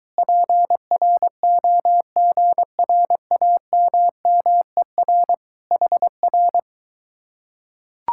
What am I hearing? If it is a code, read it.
Morse code, heard as a keyed tone, about 23 wpm, 700 Hz. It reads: PROGRAMMER HR